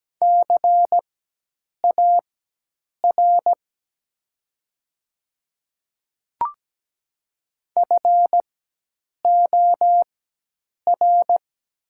Morse code, heard as a keyed tone, about 17 wpm, spaced as for 8 wpm, 700 Hz. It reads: CAR FOR
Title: 4x Character Spacing - Top 300 Words - 17wpm